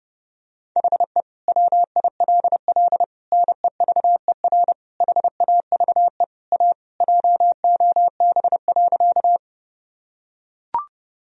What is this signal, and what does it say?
Morse code, heard as a keyed tone, about 30 wpm, 700 Hz. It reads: HE WILL NEVER HAVE A JOB.
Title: Sentences from Top 1000 Words - Review Part 3 - 30wpm